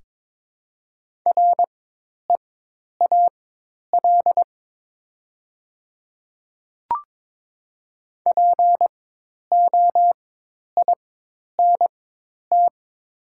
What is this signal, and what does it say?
Morse code, heard as a keyed tone, about 22 wpm, spaced as for 10 wpm, 700 Hz. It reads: REAL POINT
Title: 4x Character Spacing - Top 500 Words - 22wpm